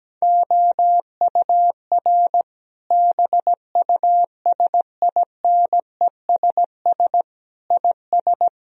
Morse code, heard as a keyed tone, about 17 wpm, 700 Hz. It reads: OUR BUSINESS IS
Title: Sentences from Top 1000 Words - Review Part 4 - 17wpm